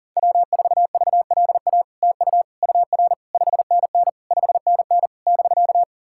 Morse code, heard as a keyed tone, about 40 wpm, 700 Hz. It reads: W4VLA TU UR 5NN 5NN <BK>
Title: SOTA QSO Practice 40wpm